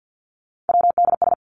Morse code, heard as a keyed tone, about 50 wpm, 700 Hz. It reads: WDS
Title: Abbreviations 50wpm